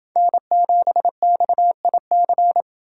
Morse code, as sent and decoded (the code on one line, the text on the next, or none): -. --... -..- .. -.-.
N7XIC